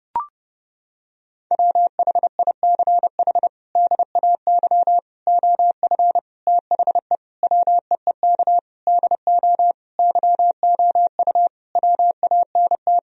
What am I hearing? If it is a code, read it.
Morse code, heard as a keyed tone, about 32 wpm, 700 Hz. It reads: WHICH DAY OF THE WEEK DO YOU WANT
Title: Sentences from Top 1000 Words - Review Part 3 - 30wpm